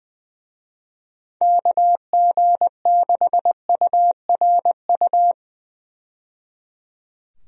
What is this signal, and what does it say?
Morse code, heard as a keyed tone, about 20 wpm, 700 Hz. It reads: KG6URU
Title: USA Call Signs - 6 Characters - 20wpm